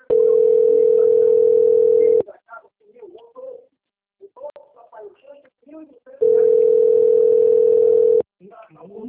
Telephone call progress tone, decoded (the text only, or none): ringback tone